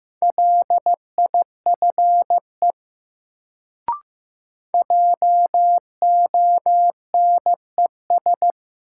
Morse code, heard as a keyed tone, about 15 wpm, 700 Hz. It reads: LIFE JONES
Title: Sets of 5 Words - New Word Drill - 15wpm